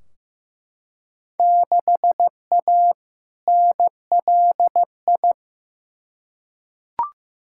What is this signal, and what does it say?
Morse code, heard as a keyed tone, about 15 wpm, 700 Hz. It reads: 6A NLI
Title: Field Day Exchanges Encore - 15wpm